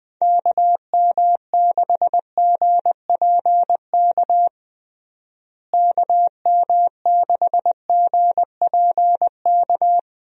Morse code, heard as a keyed tone, about 20 wpm, 700 Hz. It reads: KM6GPK KM6GPK
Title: USA Call Signs - 6 Characters - Repeated 2x - 20wpm